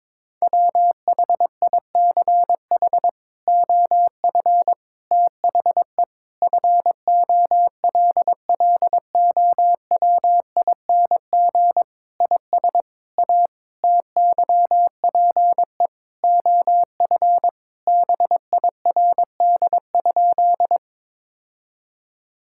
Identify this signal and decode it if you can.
Morse code, heard as a keyed tone, about 22 wpm, 700 Hz. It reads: WHICH OF THE FOLLOWING IS A TYPE OF BIRD?
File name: practice / 22wpm / Sentences with 8-Word Phrases - 22wpm